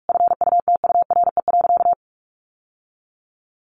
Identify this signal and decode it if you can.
Morse code, heard as a keyed tone, about 44 wpm, 700 Hz. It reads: FUTURE.
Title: Sentences from Top 700 Words 45wpm